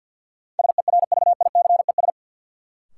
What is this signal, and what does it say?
Morse code, heard as a keyed tone, about 53 wpm, 700 Hz. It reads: SERVICES
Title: Sets of 5 Words - New Word Drill - 50wpm